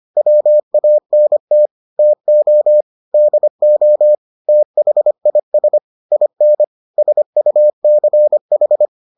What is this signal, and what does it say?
Morse code, heard as a keyed tone, about 25 wpm, 600 Hz. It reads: WANT TO DO THIS IN SUCH